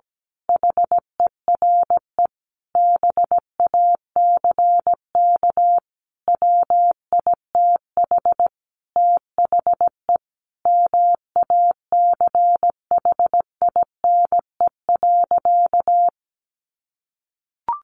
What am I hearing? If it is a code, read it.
Morse code, heard as a keyed tone, about 17 wpm, 700 Hz. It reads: HERE BACK WITH THE MACHINE.
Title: A Sound Of Thunder 17wpm